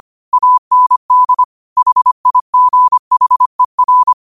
Morse code, heard as a keyed tone, about 25 wpm, 1000 Hz. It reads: AND HIGHER
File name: qotdfiles / 25wpm